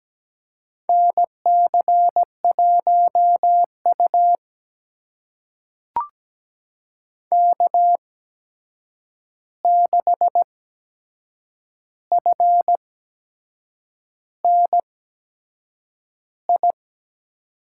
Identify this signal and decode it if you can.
Morse code, heard as a keyed tone, about 17 wpm, spaced as for 7 wpm, 700 Hz. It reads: NC1U K6FNI
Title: USA Call Signs - 8x Character Spacing - 17wpm